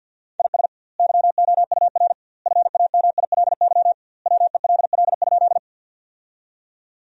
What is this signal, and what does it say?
Morse code, heard as a keyed tone, about 50 wpm, 700 Hz. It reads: IS YOUR FAMILY WELL?
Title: 2x Word-Spacing - Sentences from Top 400 Words - 50wpm